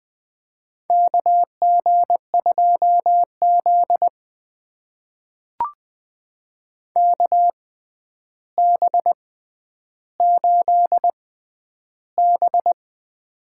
Morse code, heard as a keyed tone, about 20 wpm, spaced as for 10 wpm, 700 Hz. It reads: KG2Z KB8B
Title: USA Call Signs - 4 Characters - 6x Character Spacing - 20wpm